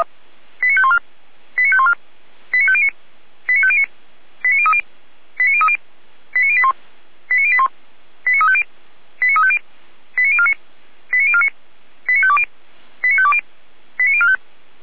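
Five-tone selective call (ZVEI), heard as quick runs of five tones, tones of about 70 ms. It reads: 89514 89514 89509 89509 89030 89030 89081 89081 89350 89350 89049 89049 89520 89520 89045
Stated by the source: ZVEI five-tone selective call received on SDR.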